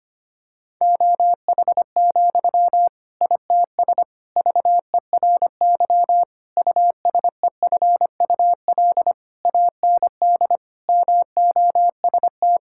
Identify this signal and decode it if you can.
Morse code, heard as a keyed tone, about 25 wpm, 700 Hz. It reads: OH, ITS VERY USEFUL AND MOST
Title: Sentences from Top 1000 Words - Review Part 6 - 25wpm